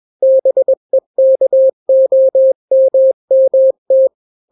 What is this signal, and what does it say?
Morse code, heard as a keyed tone, about 21 wpm, 530 Hz. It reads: BEKOMMT
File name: daily / 20wpm